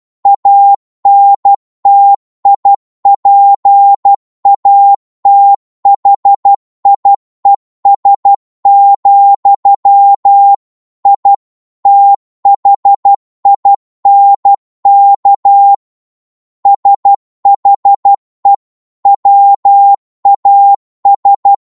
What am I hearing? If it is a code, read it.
Morse code, heard as a keyed tone, about 12 wpm, 800 Hz. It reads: ANTIPATHIES, I THINK SHE WAS
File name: Morse / psg-12wpm